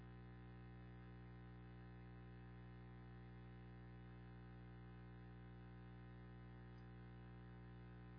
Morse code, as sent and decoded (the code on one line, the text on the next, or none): none